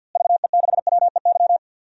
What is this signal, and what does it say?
Morse code, heard as a keyed tone, about 47 wpm, 700 Hz. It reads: VE6WEY
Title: DXCC-100 Call Signs - 6 Characters - Repeated 2x - 50wpm